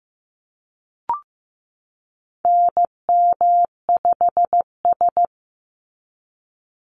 Morse code, heard as a keyed tone, about 15 wpm, 700 Hz. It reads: NM5S
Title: NA SOTA Activator Callsigns 15wpm